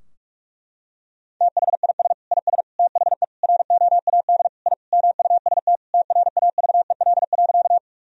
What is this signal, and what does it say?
Morse code, heard as a keyed tone, about 46 wpm, 700 Hz. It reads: THIS IS THE ROAD I MUST TRAVEL.
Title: Sentences from Top 1000 Words - Review Part 2 - 45wpm